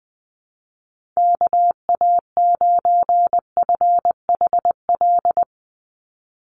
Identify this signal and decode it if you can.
Morse code, heard as a keyed tone, about 20 wpm, 700 Hz. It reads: KA9FHL
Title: USA Call Signs 20wpm